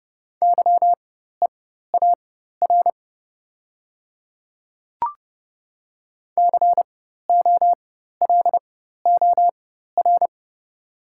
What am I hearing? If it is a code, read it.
Morse code, heard as a keyed tone, about 30 wpm, spaced as for 14 wpm, 690 Hz. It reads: YEAR COLOR
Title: 4x Character Spacing - Top 300 Words - 30wpm